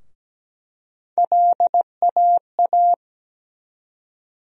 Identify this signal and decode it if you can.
Morse code, heard as a keyed tone, about 17 wpm, 700 Hz. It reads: LAA
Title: USA Call Sign Postfixes - 3 Letters 17wpm